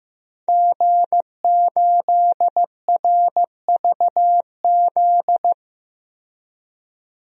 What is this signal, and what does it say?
Morse code, heard as a keyed tone, about 16 wpm, 700 Hz. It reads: G8RVZ
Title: DXCC-100 Call Signs - 5 Characters - 15wpm